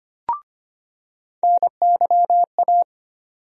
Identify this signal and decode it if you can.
Morse code, heard as a keyed tone, about 25 wpm, 700 Hz. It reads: NYA
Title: USA Call Sign Postfixes - 3 Letters 25wpm